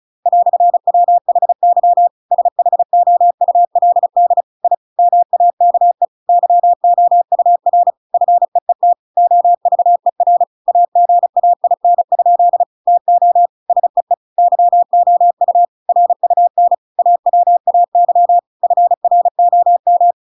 Morse code, heard as a keyed tone, about 36 wpm, 700 Hz. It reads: "WHY SHOULD I MAKE YOUR FEET OVER AGAIN? TO SEE YOU RUN AWAY FROM
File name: Pinocchio / Pinocchio ch8 35 35 ik1hge